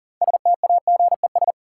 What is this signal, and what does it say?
Morse code, heard as a keyed tone, about 40 wpm, 700 Hz. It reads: STAGES
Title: Sets of 5 Words Encore - New Word Drill - 40wpm